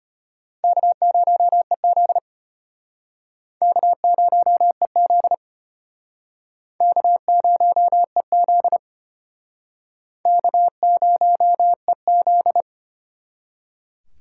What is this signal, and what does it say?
Morse code, heard as a keyed tone, about 31 wpm, 700 Hz. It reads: K0EZ K0EZ K0EZ K0EZ
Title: Speed Racing - USA 1x2 Call Signs 25wpm